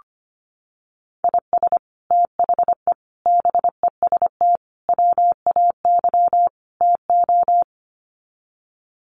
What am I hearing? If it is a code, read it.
Morse code, heard as a keyed tone, about 25 wpm, 700 Hz. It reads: IS THE BEST WAY TO
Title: Sets of 5 Words Encore 25wpm